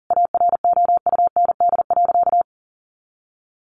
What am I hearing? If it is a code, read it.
Morse code, heard as a keyed tone, about 40 wpm, 700 Hz. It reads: AROUND.
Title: Additional Easy Phrases 40wpm